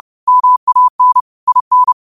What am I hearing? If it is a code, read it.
Morse code, heard as a keyed tone, about 30 wpm, 1000 Hz. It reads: MAN IN